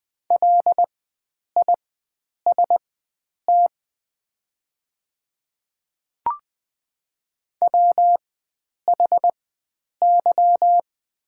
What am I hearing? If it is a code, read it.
Morse code, heard as a keyed tone, about 20 wpm, spaced as for 9 wpm, 700 Hz. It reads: LIST WHY
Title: 4x Character Spacing - Top 500 Words - 20wpm